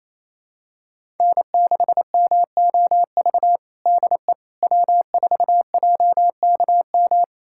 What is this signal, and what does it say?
Morse code, heard as a keyed tone, about 27 wpm, 700 Hz. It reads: N6MOV DE W4JKM
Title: USA Callsign DE Callsign - 5 Characters - 28wpm